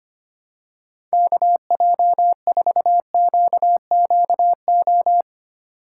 Morse code, heard as a keyed tone, about 25 wpm, 700 Hz. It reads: KJ4QQO
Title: USA Call Signs - 6 Characters - 4x Character Spacing - 25wpm